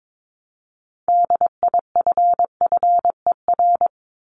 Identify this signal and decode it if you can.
Morse code, heard as a keyed tone, about 22 wpm, 700 Hz. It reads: DIFFER
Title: Top 500 Words 22wpm